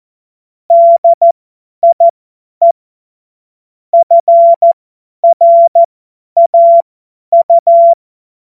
Morse code, heard as a keyed tone, about 14 wpm, spaced as for 10 wpm, 680 Hz. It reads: DIE FRAU
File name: Morsekurs / Witze / Witz 18. Juli 2023 50 F70